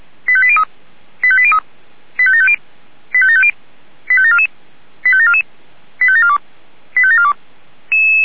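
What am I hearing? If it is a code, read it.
Five-tone selective call (ZVEI), heard as quick runs of five tones, tones of about 70 ms. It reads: 85803 85803 85750 85750 85744 85744 85742 85742
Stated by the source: ZVEI five-tone selective call received on an SDR.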